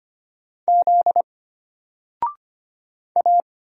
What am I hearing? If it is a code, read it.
Morse code, heard as a keyed tone, about 25 wpm, 700 Hz. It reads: Z A
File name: Single-Letter-Number-25wpm